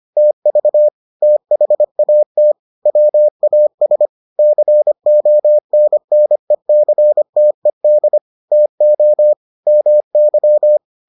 Morse code, heard as a keyed tone, about 25 wpm, 600 Hz. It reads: TV THAT WAS CONNECTED TO MY